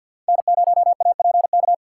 Morse code, heard as a keyed tone, about 52 wpm, 700 Hz. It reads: N0APK
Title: USA Call Signs - 6x Character Spacing - 50wpm